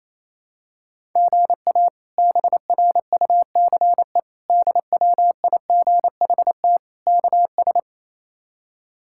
Morse code, heard as a keyed tone, about 28 wpm, 700 Hz. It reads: GA BRUCE DWIGHT KS
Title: K1USN SST QSO Practice - 28wpm